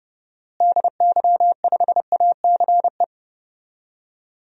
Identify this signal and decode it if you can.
Morse code, heard as a keyed tone, about 29 wpm, 700 Hz. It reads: DY5ACE
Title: DXCC-100 Call Signs - 6 Characters - Repeated 2x - 30wpm